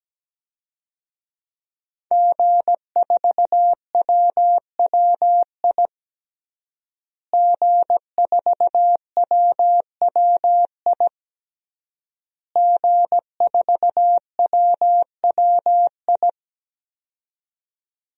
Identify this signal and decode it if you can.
Morse code, heard as a keyed tone, about 17 wpm, 700 Hz. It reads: G4WWI G4WWI G4WWI
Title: DXCC-100 Call Signs - 5 Characters - Repeated 3x - 17wpm